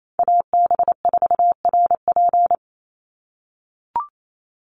Morse code, heard as a keyed tone, about 28 wpm, 700 Hz. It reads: AB4RP
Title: USA Call Signs 28wpm